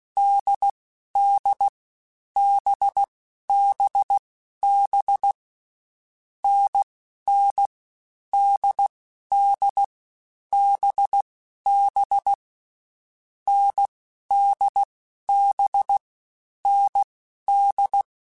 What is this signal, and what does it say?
Morse code, heard as a keyed tone, about 16 wpm, 800 Hz. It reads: D D B B B N N D D B B N D B N D